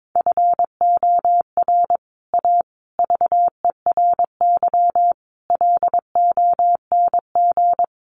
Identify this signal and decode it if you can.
Morse code, heard as a keyed tone, about 22 wpm, 700 Hz. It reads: FOR A VERY LONG
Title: Sets of 5 Words 22wpm